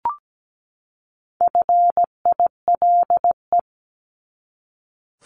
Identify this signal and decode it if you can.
Morse code, heard as a keyed tone, about 17 wpm, 700 Hz. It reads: FILE
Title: Top 1001-1250 Words 17wpm